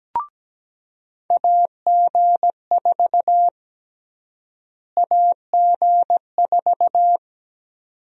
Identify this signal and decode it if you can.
Morse code, heard as a keyed tone, about 17 wpm, 700 Hz. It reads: AG4 AG4
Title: USA Call Sign Prefixes - Repeated 2x - 17wpm